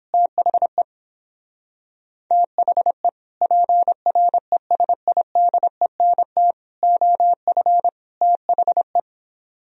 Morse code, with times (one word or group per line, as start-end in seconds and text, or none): 0.1-0.8 THE
2.3-3.1 THE
3.4-6.5 PRESIDENT
6.8-7.9 OF
8.2-9.0 THE